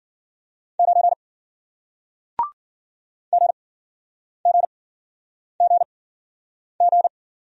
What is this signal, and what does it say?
Morse code, heard as a keyed tone, about 53 wpm, spaced as for 9 wpm, 700 Hz. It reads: 9 GGGG